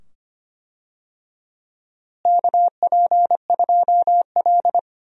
Morse code, heard as a keyed tone, about 25 wpm, 700 Hz. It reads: KP2L
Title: DXCC-100 Call Signs - 4 Characters - Repeated 2x - 25wpm